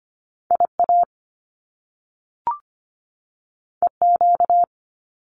Morse code, heard as a keyed tone, about 25 wpm, 700 Hz. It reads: IA EQ
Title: Top Two Letter Combinations in Words 25wpm